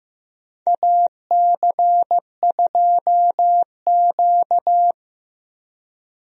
Morse code, heard as a keyed tone, about 15 wpm, 700 Hz. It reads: AC2Q
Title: USA Call Signs - 4 Characters - Repeated 2x - 15wpm